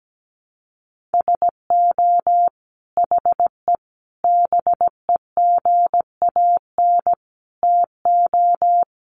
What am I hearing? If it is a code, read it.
Morse code, heard as a keyed tone, about 17 wpm, 700 Hz. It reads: SO HE BEGAN TO
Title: Sentences from Top 300 Words 17wpm